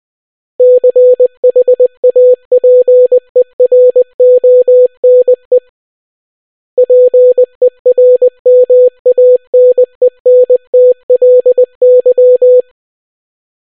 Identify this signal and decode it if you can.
Morse code, heard as a keyed tone, about 20 wpm, 500 Hz. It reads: CHAPERONE PERMANENTLY